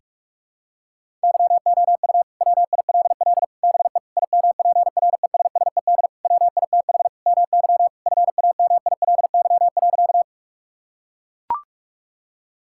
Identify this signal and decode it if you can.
Morse code, heard as a keyed tone, about 41 wpm, 700 Hz. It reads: YOU WILL BE IMPRESSED WITH MY FAMILY.
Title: Sentences from Top 1600 Words 45wpm